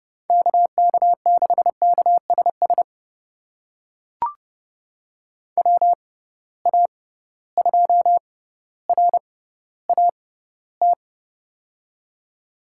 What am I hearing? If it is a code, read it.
Morse code, heard as a keyed tone, about 30 wpm, spaced as for 15 wpm, 700 Hz. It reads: KK6KSS WA2RAT
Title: USA Call Signs - 6 Characters - 6x Character Spacing - 30wpm